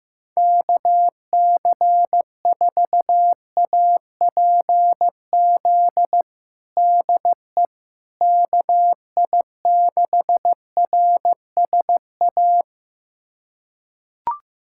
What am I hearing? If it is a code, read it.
Morse code, heard as a keyed tone, about 16 wpm, 700 Hz. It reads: KC4APZ DE KI6RSA
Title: USA Callsign DE Callsign - 6 Characters - 15wpm